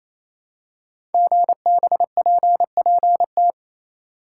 Morse code, heard as a keyed tone, about 29 wpm, 700 Hz. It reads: GBPPT
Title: Code Groups - 5 Letters - 28wpm